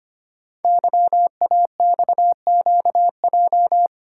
Morse code, heard as a keyed tone, about 25 wpm, 700 Hz. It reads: YAXQJ
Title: Code Groups - 5 Letters - 25wpm